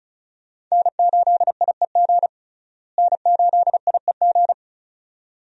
Morse code, heard as a keyed tone, about 34 wpm, 700 Hz. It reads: N8IEG N8IEG
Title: USA Call Signs - 5 Characters - Repeated 2x - 35wpm